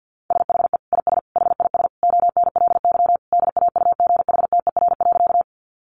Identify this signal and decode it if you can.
Morse code, heard as a keyed tone, about 50 wpm, 700 Hz. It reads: SHE IS HIS ONLY DAUGHTER.
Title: Sentences from Top 1500 Words 50wpm